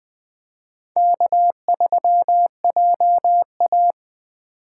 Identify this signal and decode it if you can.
Morse code, heard as a keyed tone, about 20 wpm, 700 Hz. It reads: K3JA
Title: 2023 POTA Activator Call Signs - 20wpm